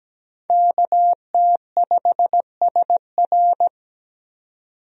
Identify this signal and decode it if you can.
Morse code, heard as a keyed tone, about 17 wpm, 700 Hz. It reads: KT5SR
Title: USA Call Signs - 5 Characters - 17wpm